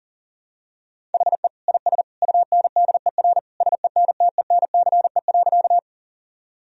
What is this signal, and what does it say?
Morse code, heard as a keyed tone, about 37 wpm, 700 Hz. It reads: HE IS UNDER SENTENCE.
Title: Sentences from Top 1000 Words - Review Part 1 - 40wpm